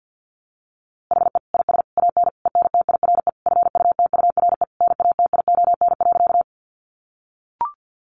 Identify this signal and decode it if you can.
Morse code, heard as a keyed tone, about 48 wpm, 700 Hz. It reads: HE IS AN ENTIRE FUTURE NATION.
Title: A Sound Of Thunder 50wpm